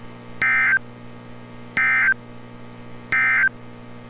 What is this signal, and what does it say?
Emergency Alert System (SAME) tones, heard as harsh buzzing bursts. It is heard from 0.3 to 3.6 s.
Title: NOAA Weather Radio - Altoona, PA